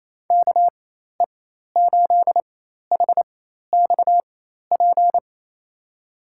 Morse code, heard as a keyed tone, about 28 wpm, spaced as for 13 wpm, 700 Hz. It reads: KE8HXP
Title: USA Call Signs - 6 Characters - 4x Character Spacing - 28wpm